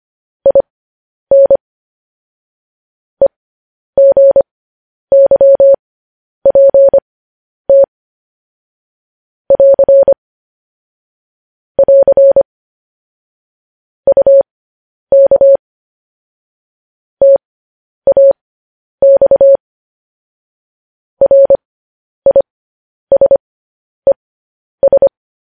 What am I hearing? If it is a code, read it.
Morse code, heard as a keyed tone, about 25 wpm, spaced as for 10 wpm, 550 Hz. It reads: IN EGYPT <AR> <AR> UK TAX RISES
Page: news-headlines-in-morse-code-at-10-wpm